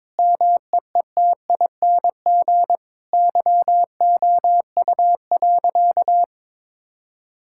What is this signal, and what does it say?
Morse code, heard as a keyed tone, about 22 wpm, 700 Hz. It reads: MEETING YOU.
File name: Sentences with 7-Word Phrases - 22wpm